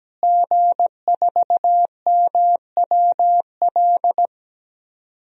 Morse code, heard as a keyed tone, about 17 wpm, 700 Hz. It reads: G4MWL
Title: DXCC-100 Call Signs - 5 Characters - Repeated 2x - 17wpm